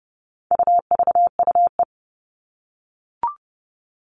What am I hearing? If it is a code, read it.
Morse code, heard as a keyed tone, about 30 wpm, 700 Hz. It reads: UVUE
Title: All Cut Numbers - 4-Digit 30wpm